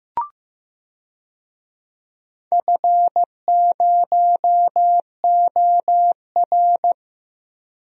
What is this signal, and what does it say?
Morse code, heard as a keyed tone, about 16 wpm, 700 Hz. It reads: F0OR
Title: DXCC-100 Call Signs - 4 Characters - Repeated 2x - 15wpm